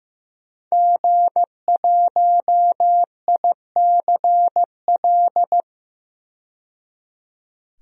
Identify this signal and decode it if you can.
Morse code, heard as a keyed tone, about 15 wpm, 700 Hz. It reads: G1ICL
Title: DXCC-100 Call Signs - 5 Characters - 15wpm